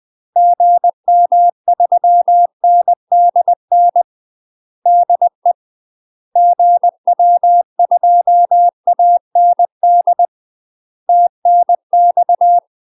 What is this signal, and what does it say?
Morse code, heard as a keyed tone, about 20 wpm, 700 Hz. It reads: GM3NDN DE GW2AND TNX
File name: UK MASTER QSOs TRAINING / 20WPM / UK MASTER QSOs 0002